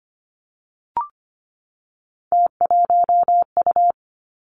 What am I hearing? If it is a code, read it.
Morse code, heard as a keyed tone, about 25 wpm, 700 Hz. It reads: T1U
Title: Three Challenging Letter-Number Combinations 25wpm